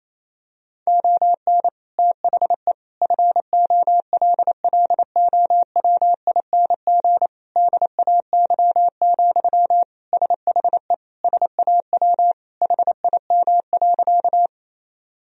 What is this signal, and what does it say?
Morse code, heard as a keyed tone, about 28 wpm, 700 Hz. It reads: ON THE FOLLOWING DAY, SHE SAW HIM.
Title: Sentences from Top 1000 Words - Review Part 4 - 28wpm